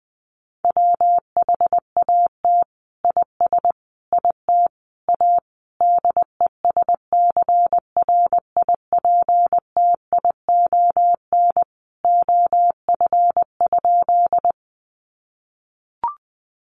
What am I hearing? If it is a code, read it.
Morse code, heard as a keyed tone, about 20 wpm, 700 Hz. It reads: WHAT IS IT A DESCRIPTION OF?
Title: Sentences from Top 1300 Words 20wpm